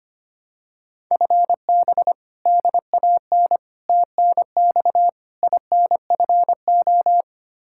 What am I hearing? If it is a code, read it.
Morse code, heard as a keyed tone, about 25 wpm, 700 Hz. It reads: FB DAN TNX INFO
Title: Practice QSO - Exchange 3 - More Info - 25wpm